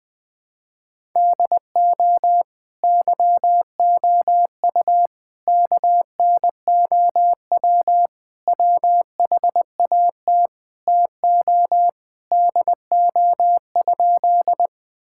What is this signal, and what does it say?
Morse code, heard as a keyed tone, about 20 wpm, 700 Hz. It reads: DO YOU KNOW WHAT TO DO?
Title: Sentences with 2-Word Phrases - 20wpm